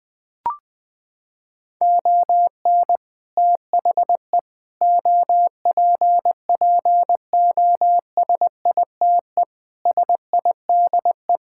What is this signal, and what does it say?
Morse code, heard as a keyed tone, about 20 wpm, 700 Hz. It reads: ON THE OPPOSITE SIDE
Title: Sets of 6 Words - 20wpm